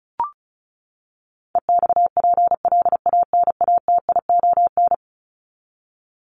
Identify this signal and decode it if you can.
Morse code, heard as a keyed tone, about 34 wpm, 700 Hz. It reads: EXPLANATION
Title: Top 1751-2000 Words 35wpm